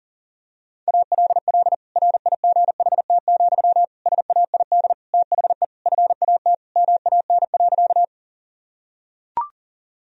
Morse code, heard as a keyed tone, about 40 wpm, 700 Hz. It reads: ALL RIGHT, SAID THE FAT MAN.
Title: Sentences from Top 1000 Words - Review Part 4 - 40wpm